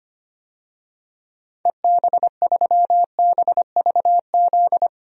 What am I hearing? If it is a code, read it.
Morse code, heard as a keyed tone, about 25 wpm, 700 Hz. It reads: EB3BVZ
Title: DXCC-100 Call Signs - 6 Characters - 25wpm